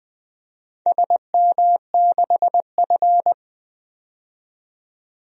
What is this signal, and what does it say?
Morse code, heard as a keyed tone, about 20 wpm, 700 Hz. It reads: SM6F
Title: DXCC-100 Call Signs - 4 Characters - 20wpm